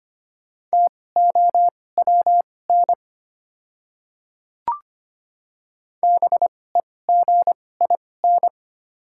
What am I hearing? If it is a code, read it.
Morse code, heard as a keyed tone, about 25 wpm, spaced as for 18 wpm, 690 Hz. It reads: TOWN BEGIN
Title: 2x Character Spacing - Top 500 Words - 25wpm